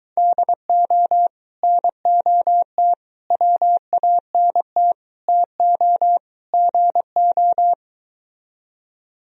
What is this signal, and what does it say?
Morse code, heard as a keyed tone, about 23 wpm, 690 Hz. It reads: DO NOT WANT TO GO